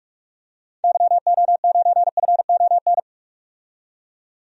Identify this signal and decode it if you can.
Morse code, heard as a keyed tone, about 46 wpm, 700 Hz. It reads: YO9FON
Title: DXCC-100 Call Signs - 6 Characters - 45wpm